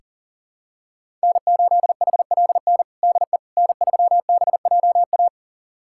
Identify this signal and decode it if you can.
Morse code, heard as a keyed tone, about 40 wpm, 700 Hz. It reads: N8HLN DE N3BJA
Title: USA Callsign DE Callsign - 5 Characters - 40wpm